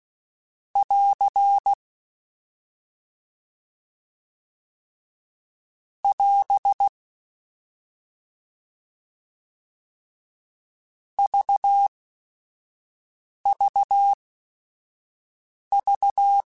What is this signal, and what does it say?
Morse code, heard as a keyed tone, about 16 wpm, spaced as for 5 wpm, 800 Hz. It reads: <AR> <AS> VVV